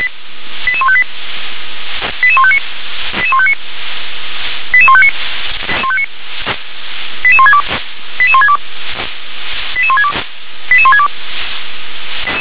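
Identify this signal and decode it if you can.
Five-tone selective call (ZVEI), heard as quick runs of five tones, tones of about 70 ms. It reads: 88158 88159 88159 88159 88162 88162 88162 88162
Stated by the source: Five-tone ZVEI call 88020 received on an SDR.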